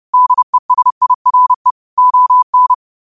Morse code, heard as a keyed tone, about 30 wpm, 1000 Hz. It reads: DESIRE ON